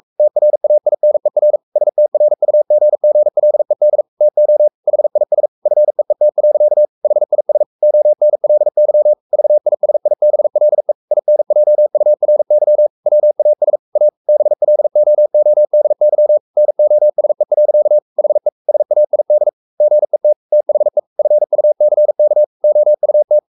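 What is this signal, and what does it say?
Morse code, heard as a keyed tone, about 44 wpm, 600 Hz. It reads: TRAINER STRUGGLED TO HIS FEET. HIS ONLY VISIBLE INJURY WAS A BLOODY NOSE. HE SAID GET THE FUCK OUT
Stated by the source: Morse code at 43 wpm sending Ken Follett's novel Never in English, from CW med Gnister.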